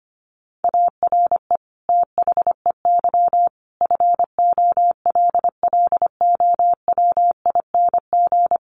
Morse code, heard as a keyed tone, about 25 wpm, 700 Hz. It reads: ARE THEY FOLLOWING
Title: Sentences from Top 700 Words 25wpm